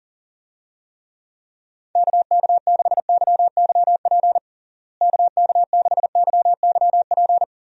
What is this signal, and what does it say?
Morse code, heard as a keyed tone, about 40 wpm, 700 Hz. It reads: KK6YYP KK6YYP
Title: USA Call Signs - 6 Characters - Repeated 2x - 40wpm